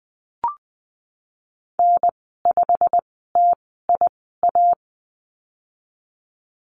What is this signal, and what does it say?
Morse code, heard as a keyed tone, about 20 wpm, spaced as for 14 wpm, 700 Hz. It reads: N5TIA
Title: 161 Lesson 26g - Letter g - Call Signs 20wpm